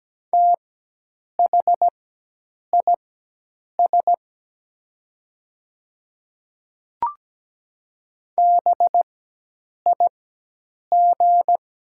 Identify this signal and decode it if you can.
Morse code, heard as a keyed tone, about 17 wpm, spaced as for 8 wpm, 700 Hz. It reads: THIS BIG
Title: 4x Character Spacing - Top 200 Words - 17wpm